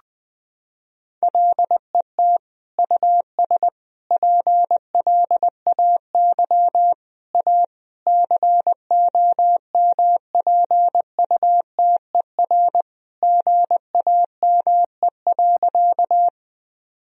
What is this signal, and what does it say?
Morse code, heard as a keyed tone, about 21 wpm, 700 Hz. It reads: LET US PLAY A COMPUTER GAME.
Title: Sentences from Top 1000 Words - Review Part 3 - 20wpm